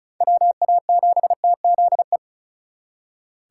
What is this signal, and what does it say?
Morse code, heard as a keyed tone, about 43 wpm, 700 Hz. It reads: WA7TZE
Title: Speed Racing - USA Call Signs 35wpm